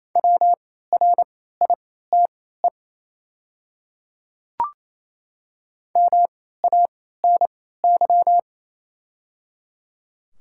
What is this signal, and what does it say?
Morse code, heard as a keyed tone, about 27 wpm, spaced as for 16 wpm, 690 Hz. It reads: WRITE MANY
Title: 3x Character Spacing - Top 100 Words - 28wpm